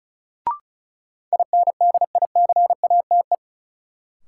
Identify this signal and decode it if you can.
Morse code, heard as a keyed tone, about 36 wpm, 700 Hz. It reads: INDICATE
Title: Sentences from Top 1300 Words - New Word Drill - 35wpm